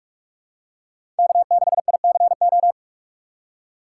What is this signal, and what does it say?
Morse code, heard as a keyed tone, about 44 wpm, 700 Hz. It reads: K6ICO
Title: USA Call Signs - 6x Character Spacing - 45wpm